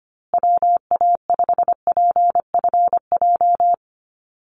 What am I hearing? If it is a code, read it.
Morse code, heard as a keyed tone, about 25 wpm, 700 Hz. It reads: WA5PFJ
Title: Call Signs Repeated 3x - 25wpm